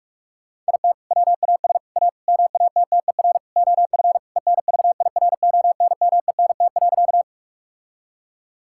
Morse code, heard as a keyed tone, about 46 wpm, 700 Hz. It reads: IT WAS A MATTER OF ENVIRONMENT.
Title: Sentences from Top 1000 Words - Review Part 5 - 45wpm